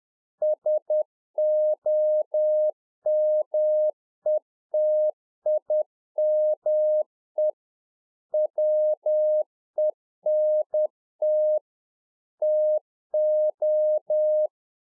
Morse code, heard as a keyed tone, about 10 wpm, 610 Hz. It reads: SOMETIME WENT TO